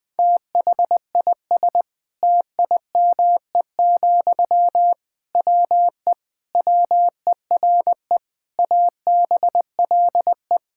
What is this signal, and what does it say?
Morse code, heard as a keyed tone, about 21 wpm, 700 Hz. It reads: THIS TIME, WE WERE ABLE
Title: Sentences from Top 1000 Words - Review Part 6 - 20wpm